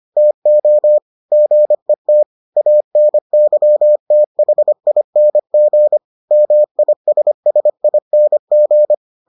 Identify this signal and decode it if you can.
Morse code, heard as a keyed tone, about 25 wpm, 600 Hz. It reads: TO GET ANYTHING MISSING